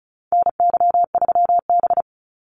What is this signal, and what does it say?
Morse code, heard as a keyed tone, about 34 wpm, 700 Hz. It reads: NY3B
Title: Call Signs - 35wpm